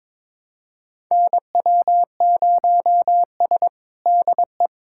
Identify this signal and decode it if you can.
Morse code, heard as a keyed tone, about 22 wpm, 700 Hz. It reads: NW0S DE
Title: USA Callsign DE Callsign - 4 Characters - 22wpm